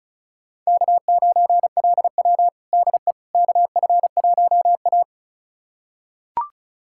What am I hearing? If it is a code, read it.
Morse code, heard as a keyed tone, about 36 wpm, 700 Hz. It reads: K9LW DE KF1A
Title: USA Callsign DE Callsign - 4 Characters - 35wpm